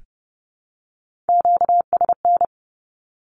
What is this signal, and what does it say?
Morse code, heard as a keyed tone, about 30 wpm, 700 Hz. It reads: QSN
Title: Q-Codes Repeated 3x 30wpm